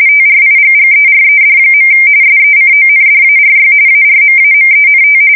RTTY, about 45.45 baud. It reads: TRIGGERED. SECONDARY ARMAMENT
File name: fsk170 / battleships